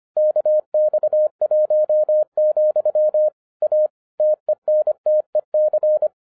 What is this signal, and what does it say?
Morse code, heard as a keyed tone, about 25 wpm, 610 Hz. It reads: KX1, A TENTEC
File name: 2011-07-mp3 / n24 QRP OTAKU 25wpm